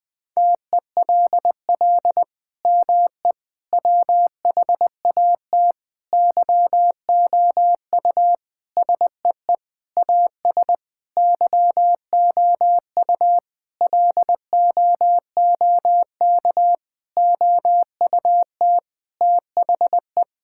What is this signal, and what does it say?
Morse code, heard as a keyed tone, about 21 wpm, 700 Hz. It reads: TELL ME WHAT YOU SEE AS YOU LOOK OUT THE
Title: Sentences from Top 1600 Words 20wpm